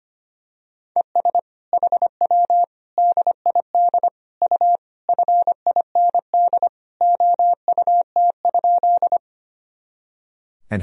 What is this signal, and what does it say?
Morse code, heard as a keyed tone, about 25 wpm, 700 Hz. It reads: ES HW DID U FIND OUT?
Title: Sentences from Top 100 Words and Abbreviations 25wpm